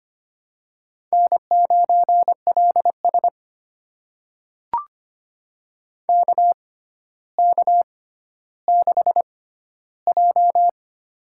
Morse code, heard as a keyed tone, about 25 wpm, spaced as for 13 wpm, 700 Hz. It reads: N9LS KK6J
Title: USA Call Signs - 4 Characters - 6x Character Spacing - 25wpm